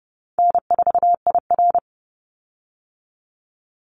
Morse code, heard as a keyed tone, about 30 wpm, 700 Hz. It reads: N4IR
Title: Easy Call Signs - 30wpm